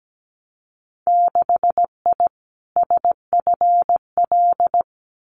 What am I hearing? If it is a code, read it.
Morse code, heard as a keyed tone, about 17 wpm, 700 Hz. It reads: 6I SFL
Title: Winter Field Day Exchanges Encore 17wpm